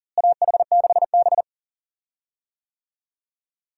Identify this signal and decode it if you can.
Morse code, heard as a keyed tone, about 40 wpm, 700 Hz. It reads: AH6B
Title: USA Call Signs - 4 Characters - Repeated 2x - 40wpm